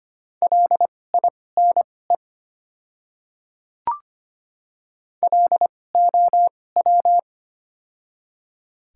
Morse code, heard as a keyed tone, about 25 wpm, spaced as for 18 wpm, 700 Hz. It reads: LINE LOW
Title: 2x Character Spacing - Top 500 Words - 25wpm